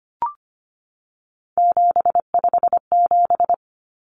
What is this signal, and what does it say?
Morse code, heard as a keyed tone, about 25 wpm, 700 Hz. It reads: 757
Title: Common Cut Numbers - 3-Digit 25wpm